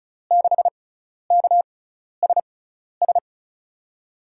Morse code, heard as a keyed tone, about 36 wpm, spaced as for 12 wpm, 690 Hz. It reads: 6KSS
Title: USA Call Signs - 6 Characters - 6x Character Spacing - 35wpm